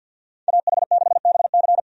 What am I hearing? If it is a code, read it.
Morse code, heard as a keyed tone, about 49 wpm, 700 Hz. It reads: AH6BC
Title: USA Call Signs - 5 Characters - 50wpm